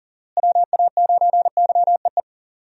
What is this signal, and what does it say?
Morse code, heard as a keyed tone, about 40 wpm, 700 Hz. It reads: WA9YEE
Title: USA Call Signs - 6 Characters - Repeated 3x - 40wpm